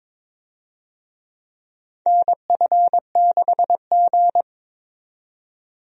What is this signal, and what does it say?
Morse code, heard as a keyed tone, about 23 wpm, 700 Hz. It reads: NF6G
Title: USA Call Signs - 4 Characters - 22wpm